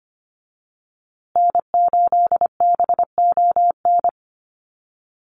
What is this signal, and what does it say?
Morse code, heard as a keyed tone, about 25 wpm, 700 Hz. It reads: N8BON
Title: USA Call Signs Repeated 2x 25wpm